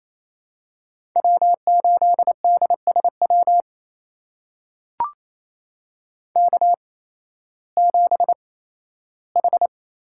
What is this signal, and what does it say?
Morse code, heard as a keyed tone, about 28 wpm, spaced as for 15 wpm, 700 Hz. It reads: W8DSW K7H
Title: USA Call Signs - 5 Characters - 8x Character Spacing - 28wpm